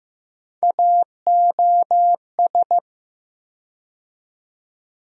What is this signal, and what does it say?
Morse code, heard as a keyed tone, about 15 wpm, 700 Hz. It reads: AOS
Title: USA Call Sign Postfixes - 3 Letters 15wpm